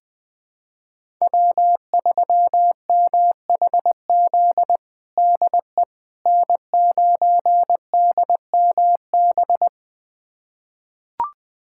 Morse code, heard as a keyed tone, about 20 wpm, 700 Hz. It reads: W3MHZ DE N9DMB
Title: USA Callsign DE Callsign - 5 Characters - 20wpm